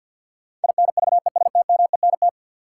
Morse code, heard as a keyed tone, about 52 wpm, 700 Hz. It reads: INVESTMENT
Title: Sentences from Top 700 Words - New Word Drill - 50wpm